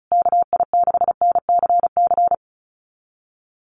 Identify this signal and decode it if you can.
Morse code, heard as a keyed tone, about 34 wpm, 700 Hz. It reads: KI6NCC
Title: USA Call Signs Repeated 3x 35wpm